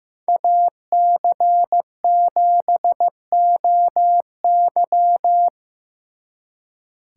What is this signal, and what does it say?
Morse code, heard as a keyed tone, about 18 wpm, 700 Hz. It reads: AC7OY
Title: Speed Racing - USA 2x2 Call Signs 15wpm